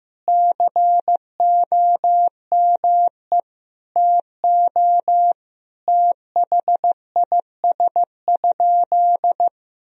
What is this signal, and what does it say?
Morse code, heard as a keyed tone, about 15 wpm, 700 Hz. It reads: COME TO THIS?
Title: Sentences from Top 100 Words and Abbreviations 15wpm